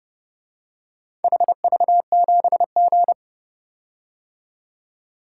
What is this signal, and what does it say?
Morse code, heard as a keyed tone, about 29 wpm, 700 Hz. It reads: HV7G